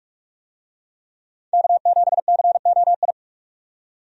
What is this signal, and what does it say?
Morse code, heard as a keyed tone, about 46 wpm, 700 Hz. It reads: K7COI
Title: USA Call Signs - 5 Characters - 45wpm